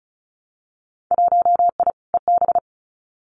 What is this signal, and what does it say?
Morse code, heard as a keyed tone, about 36 wpm, 700 Hz. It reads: 1I EB
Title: Winter Field Day Exchanges Encore 35wpm